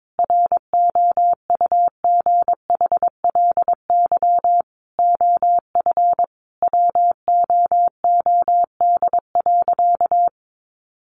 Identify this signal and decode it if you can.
Morse code, heard as a keyed tone, about 22 wpm, 700 Hz. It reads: ROUGHLY OF WOOD.
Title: Sentences from Top 1500 Words 22wpm